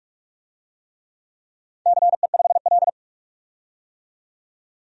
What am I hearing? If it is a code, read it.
Morse code, heard as a keyed tone, about 44 wpm, 700 Hz. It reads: CE5L
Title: DXCC-100 Call Signs - 4 Characters - 45wpm